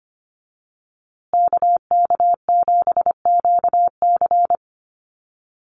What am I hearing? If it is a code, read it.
Morse code, heard as a keyed tone, about 25 wpm, 700 Hz. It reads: KK7QC
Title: USA Call Signs Repeated 2x 25wpm